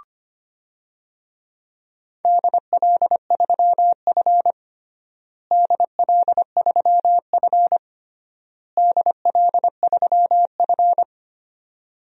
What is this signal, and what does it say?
Morse code, heard as a keyed tone, about 25 wpm, 700 Hz. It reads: DL3F DL3F DL3F
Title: DXCC-100 Call Signs - 4 Characters - Repeated 3x - 25wpm